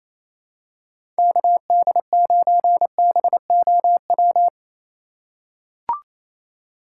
Morse code, heard as a keyed tone, about 28 wpm, 700 Hz. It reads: KD9BOW
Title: USA Call Signs - 6 Characters - 3x Character Spacing - 28wpm